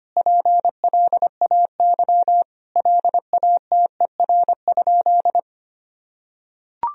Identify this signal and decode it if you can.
Morse code, heard as a keyed tone, about 25 wpm, 700 Hz. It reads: PLAY LATER?
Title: Sentences from Top 1000 Words - Review Part 4 - 25wpm